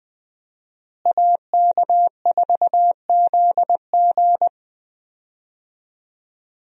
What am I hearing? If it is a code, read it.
Morse code, heard as a keyed tone, about 20 wpm, 700 Hz. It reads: AK4ZG
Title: USA Call Signs - 5 Characters - Repeated 2x - 20wpm